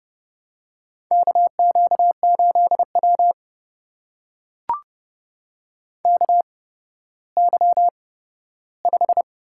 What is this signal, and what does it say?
Morse code, heard as a keyed tone, about 31 wpm, spaced as for 15 wpm, 700 Hz. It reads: KQ8W KY5
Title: USA Call Signs - 4 Characters - 8x Character Spacing - 30wpm